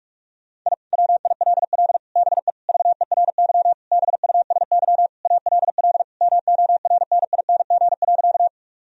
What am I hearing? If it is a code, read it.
Morse code, heard as a keyed tone, about 42 wpm, 700 Hz. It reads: I WILL BE VERY BUSY ALL MORNING.
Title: Sentences from Top 1000 Words - Review Part 3 - 45wpm